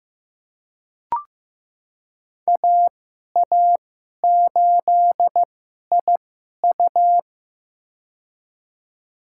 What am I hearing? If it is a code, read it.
Morse code, heard as a keyed tone, about 15 wpm, spaced as for 11 wpm, 700 Hz. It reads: AA8IU
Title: USA Call Signs - 5 Characters - 2x Character Spacing - 15wpm